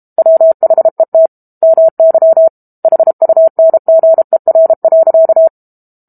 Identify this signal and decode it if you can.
Morse code, heard as a keyed tone, about 32 wpm, 650 Hz. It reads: WHET MY HUNGER.
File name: Book Two / 06-Chapter 6 - 32 WPM